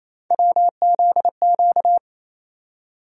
Morse code, heard as a keyed tone, about 28 wpm, 700 Hz. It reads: WZQ
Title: USA Call Sign Postfixes - 3 Letters 28wpm